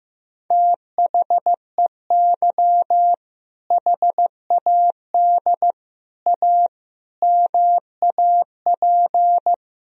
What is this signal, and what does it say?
Morse code, heard as a keyed tone, about 15 wpm, 700 Hz. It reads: THEY HAD A MAP
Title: Sentences from Top 1000 Words - Review Part 2 - 15wpm